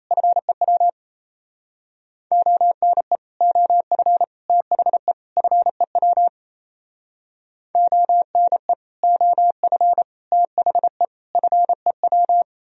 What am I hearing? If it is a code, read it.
Morse code, heard as a keyed tone, about 34 wpm, 700 Hz. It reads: FEW ONE OF THE FEW ONE OF THE FEW
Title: Speed-Racing - Sets of 4 Words 28wpm